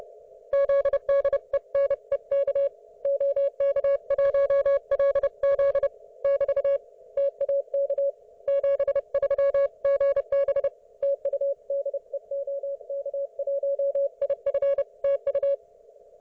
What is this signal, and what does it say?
Morse code, heard as a keyed tone, about 30 wpm, 560 Hz. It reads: ZDENEK OK1LZ <BT> TAK 73 GB TU DE OK1IF TU
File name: Archiv / CUC208 / ok1fhi cul208